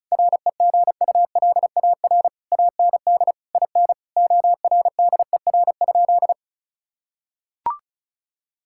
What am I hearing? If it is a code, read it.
Morse code, heard as a keyed tone, about 36 wpm, 700 Hz. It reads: REGULAR AND IN ORDER?
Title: Sentences from Top 1000 Words - Review Part 6 - 35wpm